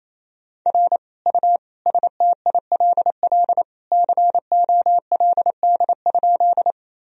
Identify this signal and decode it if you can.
Morse code, heard as a keyed tone, about 28 wpm, 700 Hz. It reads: R U STILL COLD?
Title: Sentences from Top 400 Words and Abbreviations 28wpm